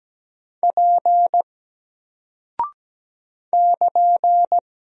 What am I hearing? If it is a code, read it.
Morse code, heard as a keyed tone, about 17 wpm, 700 Hz. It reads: P <KN>
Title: Single Character (Letter:Number:Punctuation:Prosign) Alt 17wpm